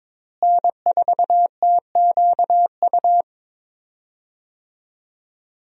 Morse code, heard as a keyed tone, about 22 wpm, 700 Hz. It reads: N4TQU
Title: USA Call Signs - 5 Characters - Repeated 2x - 22wpm